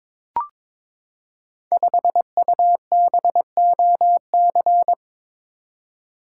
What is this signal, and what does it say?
Morse code, heard as a keyed tone, about 22 wpm, 700 Hz. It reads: 5UBOC
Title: Code Groups - 5 Letter:Numbers - 22wpm